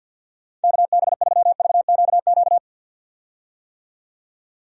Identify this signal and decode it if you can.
Morse code, heard as a keyed tone, about 52 wpm, 700 Hz. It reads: KB3VQQ
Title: USA Call Signs - 6 Characters - 50wpm